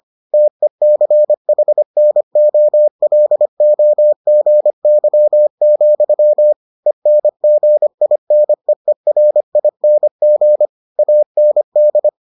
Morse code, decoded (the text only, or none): TECHNOLOGY, ENGINEERING AND